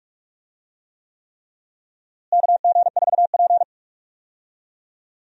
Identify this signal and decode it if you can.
Morse code, heard as a keyed tone, about 46 wpm, 700 Hz. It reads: KG4P
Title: USA Call Signs - 4 Characters - Repeated 3x - 45wpm